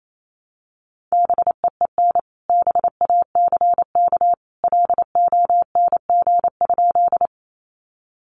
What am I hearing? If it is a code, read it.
Morse code, heard as a keyed tone, about 28 wpm, 700 Hz. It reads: BEEN BACK LONG?
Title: Sentences from Top 200 Words 28wpm